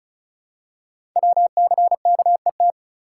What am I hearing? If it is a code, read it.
Morse code, heard as a keyed tone, about 36 wpm, 700 Hz. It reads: WCKET
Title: Code Groups - 5 Letters - 35wpm